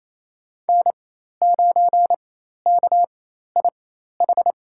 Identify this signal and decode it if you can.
Morse code, heard as a keyed tone, about 28 wpm, spaced as for 13 wpm, 700 Hz. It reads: N9KIH
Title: USA Call Signs - 4x Character Spacing - 28wpm